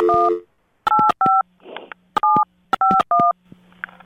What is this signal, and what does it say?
DTMF, heard as telephone touch tones, tones of about 220 ms. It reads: 185 751